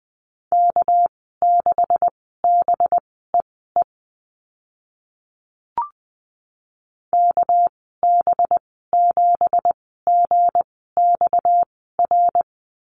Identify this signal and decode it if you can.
Morse code, heard as a keyed tone, about 20 wpm, spaced as for 14 wpm, 700 Hz. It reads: K6BEE KB7GXR
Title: 231 Lesson 37g - Letter x - Call Signs 20wpm